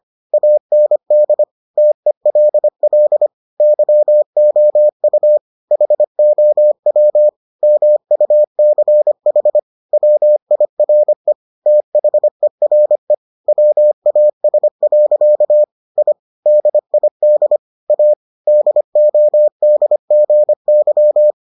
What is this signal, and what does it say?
Morse code, heard as a keyed tone, about 25 wpm, 600 Hz. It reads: AND TELL YOU HOW MUCH WIRE THERE WAS. I DID A DODGY